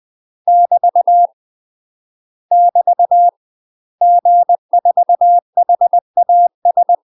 Morse code, heard as a keyed tone, about 20 wpm, 700 Hz. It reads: <BT> <BT> G4HAS
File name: UK MASTER QSOs TRAINING / 20WPM / UK MASTER QSOs 0030